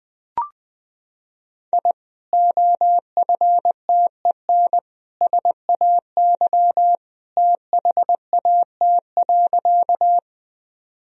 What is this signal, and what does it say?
Morse code, heard as a keyed tone, about 20 wpm, 700 Hz. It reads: I OFTEN SAY THAT.
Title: Sentences from Top 1000 Words - Review Part 1 - 20wpm